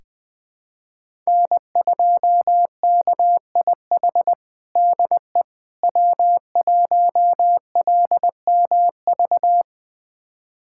Morse code, heard as a keyed tone, about 20 wpm, 700 Hz. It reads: N2KIH DE W1LMV
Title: USA Callsign DE Callsign - 5 Characters - 20wpm